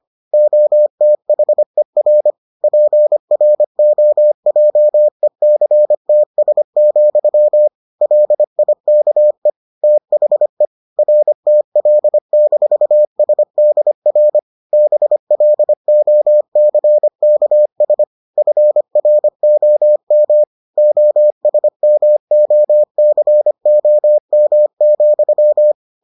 Morse code, heard as a keyed tone, about 25 wpm, 600 Hz. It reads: OTHER PROJECTS, LIKE THE RTL-SDR BLOCKS FROM OSMOCOM,